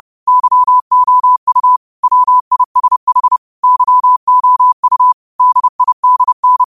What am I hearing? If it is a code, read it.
Morse code, heard as a keyed tone, about 30 wpm, 1000 Hz. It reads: YOU WISH YOU DIDN